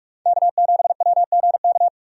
Morse code, heard as a keyed tone, about 44 wpm, 700 Hz. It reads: K7WGK
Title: USA Call Signs - 5 Characters - 45wpm